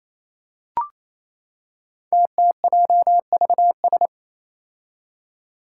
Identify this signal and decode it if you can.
Morse code, heard as a keyed tone, about 28 wpm, 700 Hz. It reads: TTJVS
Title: Five Challenging Letter-Number Combinations 28wpm